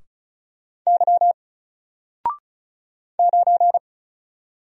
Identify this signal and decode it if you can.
Morse code, heard as a keyed tone, about 36 wpm, 700 Hz. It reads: Y 9
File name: Single-Letter-Number-35wpm